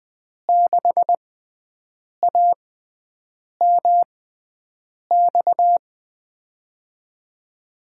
Morse code, heard as a keyed tone, about 20 wpm, spaced as for 7 wpm, 700 Hz. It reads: 6AMX
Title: USA Call Signs - 6 Characters - 6x Character Spacing - 20wpm